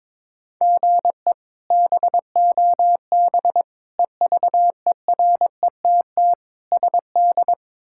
Morse code, heard as a keyed tone, about 22 wpm, 700 Hz. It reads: GE BOB EVERETT SD
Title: K1USN SST QSO Practice - 22wpm